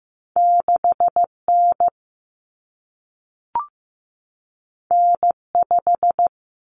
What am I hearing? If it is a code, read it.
Morse code, heard as a keyed tone, about 15 wpm, 700 Hz. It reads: 6N N5
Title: Common Cut Numbers - 2-Digit 15wpm